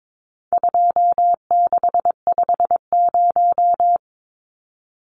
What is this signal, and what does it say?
Morse code, heard as a keyed tone, about 22 wpm, 700 Hz. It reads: 2650
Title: Numbers - 4-Digit 22wpm